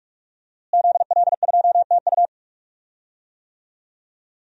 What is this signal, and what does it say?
Morse code, heard as a keyed tone, about 46 wpm, 700 Hz. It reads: ZL2TU
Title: DXCC-100 Call Signs - 5 Characters - Repeated 3x - 45wpm